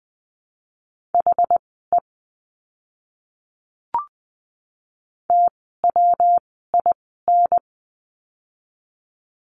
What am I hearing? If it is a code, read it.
Morse code, heard as a keyed tone, about 20 wpm, spaced as for 14 wpm, 700 Hz. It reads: HE TWIN
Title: 115 Lesson 19f - Letter w - Words 20wpm